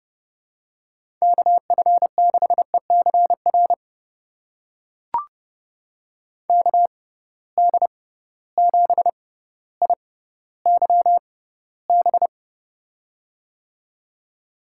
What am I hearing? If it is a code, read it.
Morse code, heard as a keyed tone, about 31 wpm, spaced as for 16 wpm, 700 Hz. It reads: KF6ECR KD7IYB
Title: USA Call Signs - 6 Characters - 6x Character Spacing - 30wpm